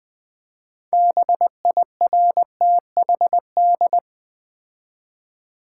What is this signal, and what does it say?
Morse code, heard as a keyed tone, about 20 wpm, 700 Hz. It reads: BIRTHD
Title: Build A Word - 2 Letters At A Time - 8-Letter Words - 20wpm